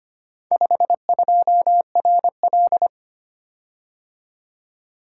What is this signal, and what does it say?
Morse code, heard as a keyed tone, about 25 wpm, 700 Hz. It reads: 52RL
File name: practice / 25wpm / Four Letter-Number Combinations - 25wpm